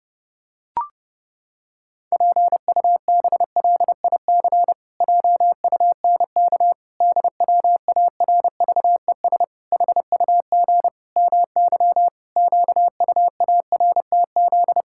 Morse code, heard as a keyed tone, about 30 wpm, 700 Hz. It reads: PUBLIC JUNK DWARVES HUG MY QUARTZ
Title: Pangram Challenge v2 - 30wpm